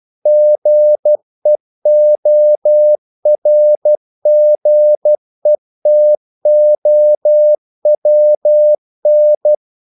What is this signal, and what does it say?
Morse code, heard as a keyed tone, about 12 wpm, 600 Hz. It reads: GEORGETOWN
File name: CW / Practice / NationalCapitals12